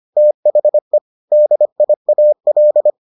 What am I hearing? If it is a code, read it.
Morse code, heard as a keyed tone, about 25 wpm, 600 Hz. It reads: THE DIAL